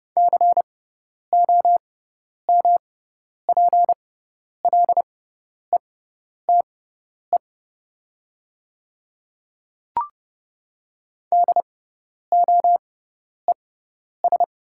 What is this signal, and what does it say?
Morse code, heard as a keyed tone, about 30 wpm, spaced as for 10 wpm, 700 Hz. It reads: COMPLETE DOES
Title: 6x Character Spacing - Top 500 Words - 30wpm